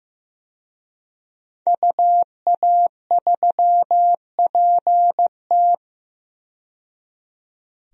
Morse code, heard as a keyed tone, about 15 wpm, 700 Hz. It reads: UA3PT
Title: DXCC-100 Call Signs - 5 Characters - 15wpm